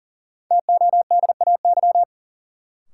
Morse code, heard as a keyed tone, about 40 wpm, 700 Hz. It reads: TODAY
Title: Sentences from Top 600 Words - New Word Drill - 40wpm